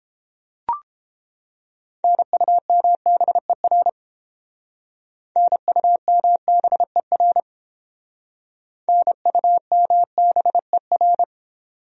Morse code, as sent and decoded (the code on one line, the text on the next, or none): -. ..- -- -... .-. / -. ..- -- -... .-. / -. ..- -- -... .-.
NUMBER NUMBER NUMBER